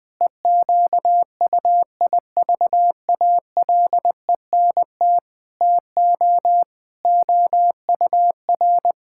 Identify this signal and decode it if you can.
Morse code, heard as a keyed tone, about 20 wpm, 700 Hz. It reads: EQUIVALENT TO OUR
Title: Sentences from Top 1600 Words 20wpm